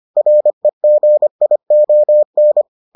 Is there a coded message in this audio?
Yes: Morse code from 0.2 to 2.6 s, heard as a keyed tone, about 25 wpm, 600 Hz. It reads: REGION